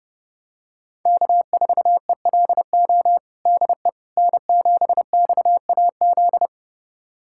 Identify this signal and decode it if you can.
Morse code, heard as a keyed tone, about 32 wpm, 700 Hz. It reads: K4ELO DE N7XAZ
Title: USA Callsign DE Callsign 30wpm